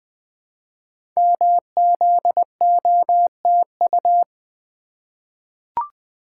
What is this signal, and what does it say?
Morse code, heard as a keyed tone, about 20 wpm, 700 Hz. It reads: MZOTU
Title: Code Groups - 5 Letters - 20wpm